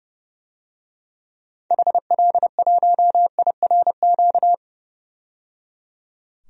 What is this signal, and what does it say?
Morse code, heard as a keyed tone, about 30 wpm, 700 Hz. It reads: HL1IRQ
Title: DXCC-100 Call Signs - 6 Characters - 30wpm